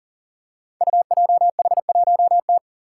Morse code, heard as a keyed tone, about 43 wpm, 700 Hz. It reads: UJH1T
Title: Five Challenging Letter-Number Combinations 40wpm